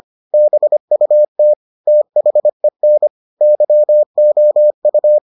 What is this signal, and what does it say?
Morse code, heard as a keyed tone, about 25 wpm, 600 Hz. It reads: BUT THEN YOU